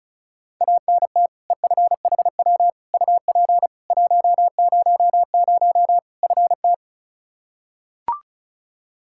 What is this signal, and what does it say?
Morse code, heard as a keyed tone, about 36 wpm, 700 Hz. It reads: ANT EFHW UP 100 FT
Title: Practice QSO - Exchange 2 - Info - 35wpm